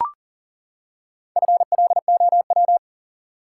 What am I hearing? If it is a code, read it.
Morse code, heard as a keyed tone, about 40 wpm, 700 Hz. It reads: FLOW
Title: Four Letter Words - 40wpm